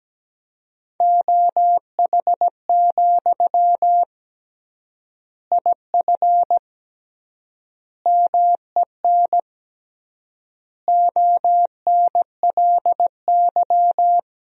Morse code, heard as a keyed tone, about 17 wpm, 700 Hz. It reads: OH, IF MEN ONLY
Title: 3x Word-Spacing - Sentences from Top 500 Words - 17wpm